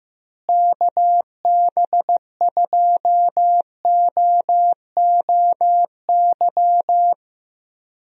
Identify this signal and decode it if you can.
Morse code, heard as a keyed tone, about 15 wpm, 700 Hz. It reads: KB2OOY
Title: USA Call Signs - 6 Characters - Repeated 2x - 15wpm